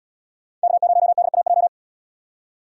Morse code, heard as a keyed tone, about 78 wpm, 690 Hz. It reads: K0DAJ